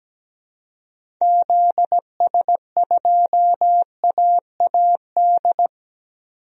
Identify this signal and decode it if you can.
Morse code, heard as a keyed tone, about 17 wpm, 700 Hz. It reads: ZS2AAD
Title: DXCC-100 Call Signs - 6 Characters - 17wpm